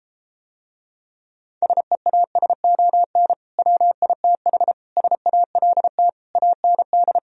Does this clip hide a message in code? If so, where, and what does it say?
Yes: Morse code from 1.6 to 7.2 s, heard as a keyed tone, about 33 wpm, 700 Hz. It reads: SEASON WITH SALT AND